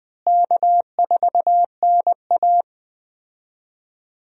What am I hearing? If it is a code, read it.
Morse code, heard as a keyed tone, about 20 wpm, 700 Hz. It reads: K4NA
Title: USA Call Signs - 4 Characters - Repeated 2x - 20wpm